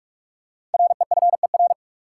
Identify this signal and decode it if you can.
Morse code, heard as a keyed tone, about 47 wpm, 700 Hz. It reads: REFER
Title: Five Letter Words - 45wpm